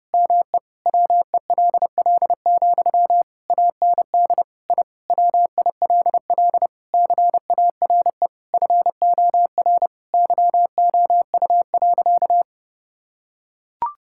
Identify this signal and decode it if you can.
Morse code, heard as a keyed tone, about 32 wpm, 700 Hz. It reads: ME WELL, AND I WILL CARE FOR YOU.
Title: Sentences from Top 1000 Words - Review Part 6 - 30wpm